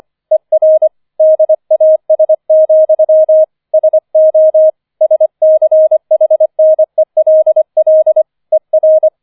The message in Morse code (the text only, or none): ER DAS, SO SCHNELL ER